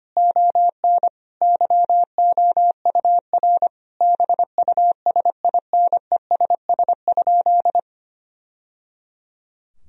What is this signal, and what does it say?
Morse code, heard as a keyed tone, about 25 wpm, 700 Hz. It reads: ON YOUR BUSINESS?
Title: Sentences from Top 1000 Words - Review Part 4 - 25wpm